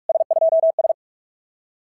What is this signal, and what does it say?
Morse code, heard as a keyed tone, about 43 wpm, 650 Hz. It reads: S2S
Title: SOTA QSO Practice 45wpm